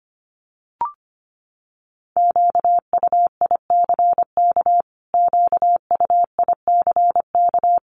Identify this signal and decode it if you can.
Morse code, heard as a keyed tone, about 25 wpm, 700 Hz. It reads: QUICK QUICK
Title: Top 500 Words Repeated 25wpm